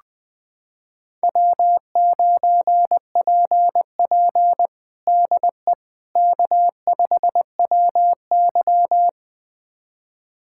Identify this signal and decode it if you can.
Morse code, heard as a keyed tone, about 20 wpm, 700 Hz. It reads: W9PP DE K5WY
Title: USA Callsign DE Callsign - 4 Characters - 20wpm